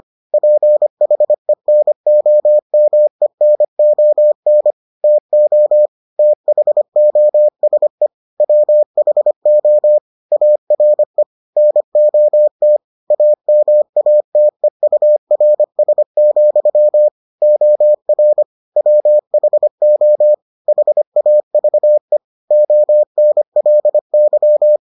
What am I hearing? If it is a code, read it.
Morse code, heard as a keyed tone, about 25 wpm, 600 Hz. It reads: PHENOMENON TO THOSE WHO ARE NOT AMATEURS, OR WHO HAVE ONLY